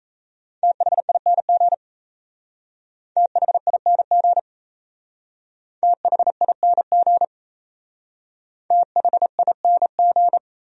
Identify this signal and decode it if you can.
Morse code, heard as a keyed tone, about 34 wpm, 700 Hz. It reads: THING THING THING THING
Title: Speed-Racing Top 200 Words 28wpm